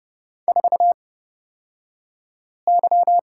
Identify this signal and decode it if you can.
Morse code, heard as a keyed tone, about 30 wpm, spaced as for 5 wpm, 700 Hz. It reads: 4Y
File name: Lesson 38d - Letter q - Two Characters 30wpm